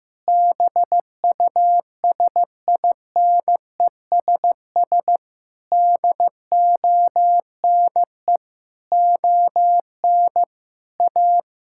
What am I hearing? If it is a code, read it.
Morse code, heard as a keyed tone, about 15 wpm, 690 Hz. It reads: BUSINESS DONE ON A
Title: Sentences from Top 1000 Words - Review Part 5 - 15wpm